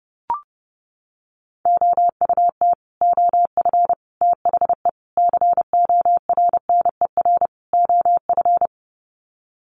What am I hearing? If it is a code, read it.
Morse code, heard as a keyed tone, about 30 wpm, 700 Hz. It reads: OUT OF THE CORNER OF
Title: Sets of 5 Words 30wpm